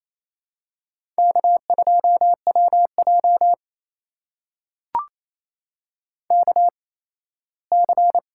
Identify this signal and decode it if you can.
Morse code, heard as a keyed tone, about 28 wpm, spaced as for 17 wpm, 700 Hz. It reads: K2WJ KC
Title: USA Call Signs - 8x Character Spacing - 28wpm